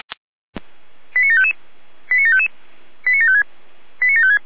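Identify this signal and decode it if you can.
Five-tone selective call (ZVEI), heard as quick runs of five tones, tones of about 70 ms. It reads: 89755 89755 89756 89756
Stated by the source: ZVEI five-tone selective call 89701 received via SDR.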